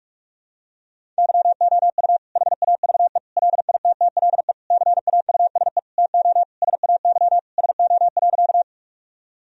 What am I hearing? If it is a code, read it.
Morse code, heard as a keyed tone, about 41 wpm, 700 Hz. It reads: YOU HAVE LITTLE CAUSE TO SAY SO.
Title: Sentences from Top 1000 Words - Review Part 1 - 45wpm